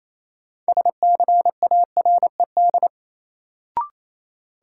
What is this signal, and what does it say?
Morse code, heard as a keyed tone, about 28 wpm, 700 Hz. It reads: SCARED
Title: Sentences from Top 700 Words - New Word Drill - 28wpm